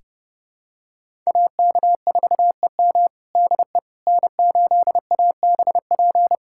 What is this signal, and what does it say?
Morse code, heard as a keyed tone, about 31 wpm, 700 Hz. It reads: AK4EM DE N8ABP
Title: USA Callsign DE Callsign - 5 Characters - 30wpm